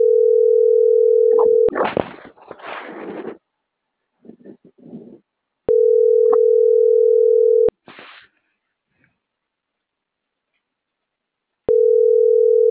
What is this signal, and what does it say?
Ringback tone (North America), 440 + 480 Hz.